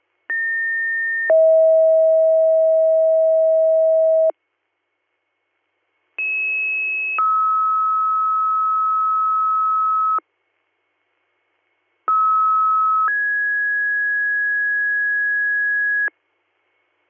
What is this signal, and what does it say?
Two-tone page, heard as a one-second tone, then a three-second tone, about 1800 then 640 Hz, about 2600 then 1290 Hz, about 1290 then 1650 Hz. It is heard from 0.3 to 16.1 s.